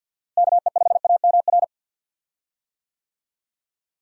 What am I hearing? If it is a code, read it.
Morse code, heard as a keyed tone, about 53 wpm, 700 Hz. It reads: KE5AMR
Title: USA Call Signs - 6 Characters - Repeated 3x - 50wpm